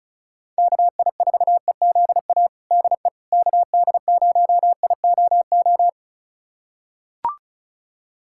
Morse code, heard as a keyed tone, about 36 wpm, 700 Hz. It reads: KI4EZA DE KD0IOO
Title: USA Callsign DE Callsign - 6 Characters - 35wpm